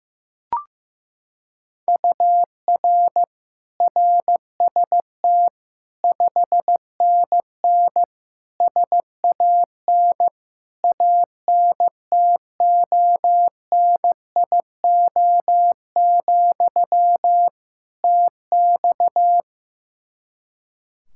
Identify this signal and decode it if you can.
Morse code, heard as a keyed tone, about 16 wpm, 700 Hz. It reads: UR RST 5NN SAN ANTONIO, TX
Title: Practice QSO - Exchange 1 - Beginning - 15wpm